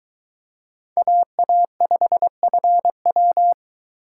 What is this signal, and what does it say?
Morse code, heard as a keyed tone, about 23 wpm, 700 Hz. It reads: AA5FW